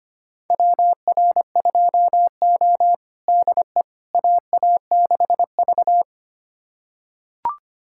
Morse code, heard as a keyed tone, about 25 wpm, 700 Hz. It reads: WR2O DE AA6V
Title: USA Callsign DE Callsign - 4 Characters - 25wpm